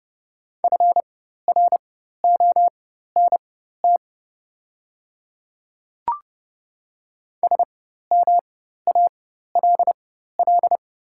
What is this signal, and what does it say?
Morse code, heard as a keyed tone, about 30 wpm, spaced as for 14 wpm, 690 Hz. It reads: FRONT SMALL
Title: 4x Character Spacing - Top 500 Words - 30wpm